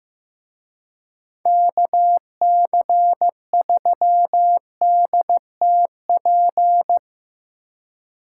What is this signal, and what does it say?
Morse code, heard as a keyed tone, about 17 wpm, 700 Hz. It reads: KC3DTP
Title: Speed Racing - USA Call Signs 15wpm